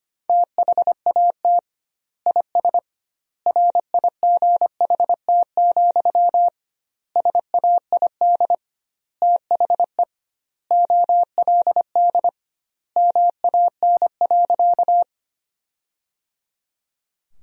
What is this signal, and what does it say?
Morse code, heard as a keyed tone, about 25 wpm, 700 Hz. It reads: THAT IS RIGHT, SAID THE OLD MAN.
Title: 2x Word-Spacing - Sentences from Top 200 Words - 25wpm